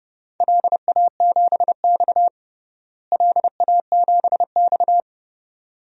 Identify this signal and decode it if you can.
Morse code, heard as a keyed tone, about 31 wpm, 700 Hz. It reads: LA7X LA7X
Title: DXCC-100 Call Signs - 4 Characters - Repeated 2x - 30wpm